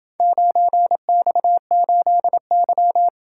Morse code, heard as a keyed tone, about 28 wpm, 700 Hz. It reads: 9X8Y